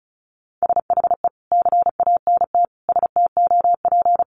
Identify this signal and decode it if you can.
Morse code, heard as a keyed tone, about 34 wpm, 700 Hz. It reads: SHE CANT STOP
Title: Easy Phrases 35wpm